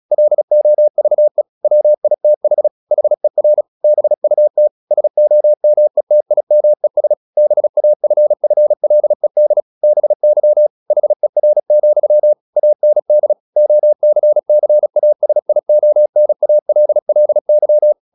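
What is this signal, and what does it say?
Morse code, heard as a keyed tone, about 36 wpm, 600 Hz. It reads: LOVE WITH HER BUT SOMETIMES BAFFLED BY HER, AND OCCASIONALLY
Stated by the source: Morse practice at 36 wpm, text of Ken Follett's novel Never, from CW med Gnister.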